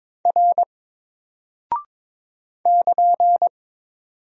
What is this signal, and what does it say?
Morse code, heard as a keyed tone, about 22 wpm, 700 Hz. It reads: R <KN>
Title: Single Character (Letter:Number:Punctuation:Prosign) 22wpm